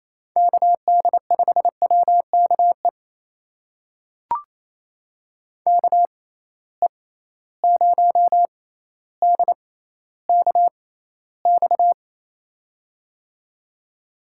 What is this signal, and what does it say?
Morse code, heard as a keyed tone, about 28 wpm, spaced as for 14 wpm, 700 Hz. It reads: KD5WKE KE0DKX
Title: USA Call Signs - 6 Characters - 6x Character Spacing - 28wpm